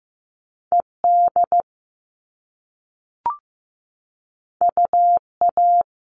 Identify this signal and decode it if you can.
Morse code, heard as a keyed tone, about 15 wpm, 700 Hz. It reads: ED UA
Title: All Cut Numbers - 2-Digit 15wpm